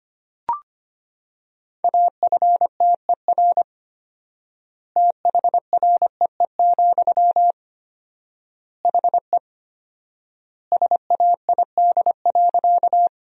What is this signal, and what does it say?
Morse code, heard as a keyed tone, about 25 wpm, 700 Hz. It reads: AFTER THREE, HE SAID.
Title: 4x Word-Spacing - Sentences from Top 200 Words - 25wpm